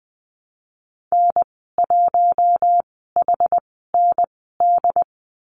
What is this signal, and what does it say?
Morse code, heard as a keyed tone, about 19 wpm, spaced as for 14 wpm, 700 Hz. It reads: N1HND
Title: 128 Lesson 21g - Number 6 - Call Signs 20wpm